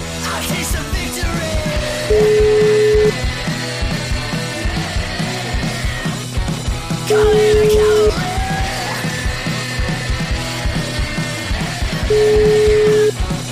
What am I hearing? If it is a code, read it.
Ringback tone (Europe, 425 Hz), with noise.
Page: Uvítací melodii uslyší volající, než přijmete hovor.